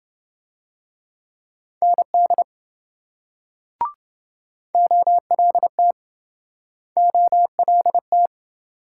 Morse code, heard as a keyed tone, about 27 wpm, 700 Hz. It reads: ND OLT OLT